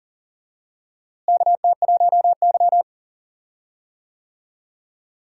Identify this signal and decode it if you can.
Morse code, heard as a keyed tone, about 40 wpm, 700 Hz. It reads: KT1Y
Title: USA Call Signs - 4 Characters - Repeated 2x - 40wpm